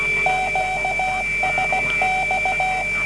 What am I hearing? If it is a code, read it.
Morse code, heard as a keyed tone, about 16 wpm, 750 Hz, with noise. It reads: QSX